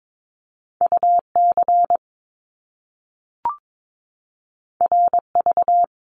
Morse code, heard as a keyed tone, about 22 wpm, 700 Hz. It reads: UC RV
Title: Top Two Letter Combinations in Words 22wpm